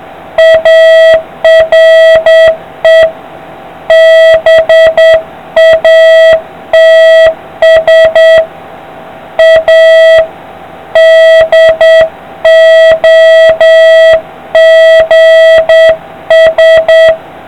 Morse code, heard as a keyed tone, about 8 wpm, 650 Hz, with noise. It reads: ARE BATS A DOGS